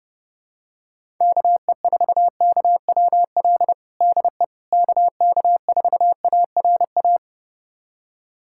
Morse code, heard as a keyed tone, about 31 wpm, 700 Hz. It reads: KE4KWL DE KK4ARA
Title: USA Callsign DE Callsign - 6 Characters - 30wpm